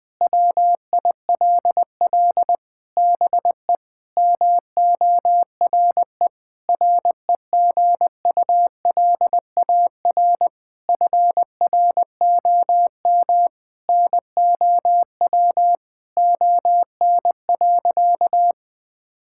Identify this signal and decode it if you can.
Morse code, heard as a keyed tone, about 20 wpm, 700 Hz. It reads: WILL BE MORE REGULAR FROM NOW ON.
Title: Sentences from Top 1000 Words - Review Part 6 - 20wpm